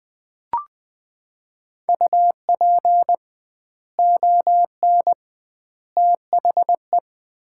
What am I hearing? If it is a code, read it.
Morse code, heard as a keyed tone, about 20 wpm, 700 Hz. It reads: UP ON THE
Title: 2x Word-Spacing - Sets of 3 Words - 20wpm